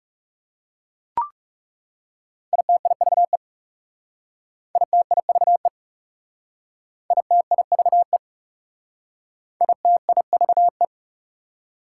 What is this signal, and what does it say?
Morse code, heard as a keyed tone, about 38 wpm, 690 Hz. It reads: ITIVE ITIVE ITIVE ITIVE
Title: Speed-Racing Prefixes and Suffixes 30wpm